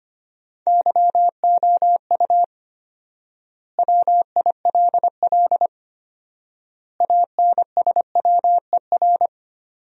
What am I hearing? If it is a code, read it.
Morse code, heard as a keyed tone, about 25 wpm, 700 Hz. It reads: YOU WILL ANSWER
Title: 4x Word-Spacing - Sentences from Top 300 Words - 25wpm